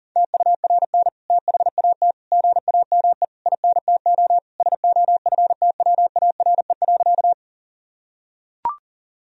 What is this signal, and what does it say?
Morse code, heard as a keyed tone, about 40 wpm, 700 Hz. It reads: TURN THAT GAME INTO SOFTWARE.
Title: Sentences from Top 1000 Words - Review Part 3 - 40wpm